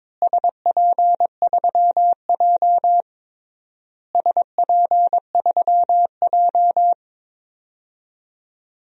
Morse code, heard as a keyed tone, about 22 wpm, 700 Hz. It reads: SP3J SP3J
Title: DXCC-100 Call Signs - 4 Characters - Repeated 3x - 22wpm